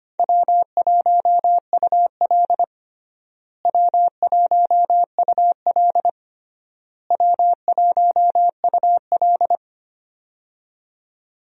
Morse code, heard as a keyed tone, about 25 wpm, 700 Hz. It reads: W1UL W1UL W1UL
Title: USA Call Signs - 4 Characters - Repeated 3x - 25wpm